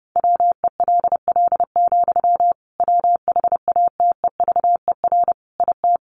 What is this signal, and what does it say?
Morse code, heard as a keyed tone, about 30 wpm, 700 Hz. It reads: WELL, WHATEVER IT
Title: Sentences from Top 1100 Words 30wpm